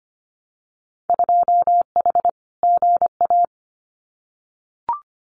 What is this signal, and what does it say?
Morse code, heard as a keyed tone, about 25 wpm, 690 Hz. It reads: 2H GA
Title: Winter Field Day Exchanges Encore 25wpm